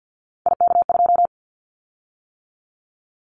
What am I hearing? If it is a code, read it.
Morse code, heard as a keyed tone, about 50 wpm, 700 Hz. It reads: IK2
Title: Callsign Prefixes for Top 50 DXCC - 50wpm